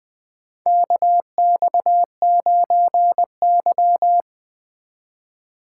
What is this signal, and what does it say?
Morse code, heard as a keyed tone, about 20 wpm, 700 Hz. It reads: KX9Y
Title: USA Call Signs - 4 Characters - 2x Character Spacing - 20wpm